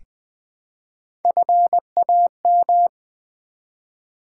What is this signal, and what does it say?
Morse code, heard as a keyed tone, about 20 wpm, 700 Hz. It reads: FAM
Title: Build A Word - 1 Letter At A Time - 3 to 5-Letter Words - 20wpm